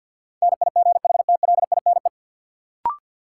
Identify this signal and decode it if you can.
Morse code, heard as a keyed tone, about 53 wpm, 700 Hz. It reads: NIGHTLINE
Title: Sets of 5 Words - New Word Drill - 50wpm